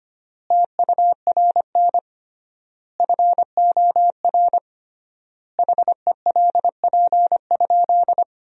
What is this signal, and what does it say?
Morse code, heard as a keyed tone, about 25 wpm, 700 Hz. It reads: TURN FOR HELP?
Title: 3x Word-Spacing - Sentences from Top 200 Words - 25wpm